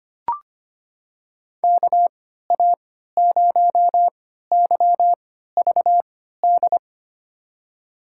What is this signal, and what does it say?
Morse code, heard as a keyed tone, about 25 wpm, spaced as for 14 wpm, 700 Hz. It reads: KA0YVD
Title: USA Call Signs - 6 Characters - 3x Character Spacing - 25wpm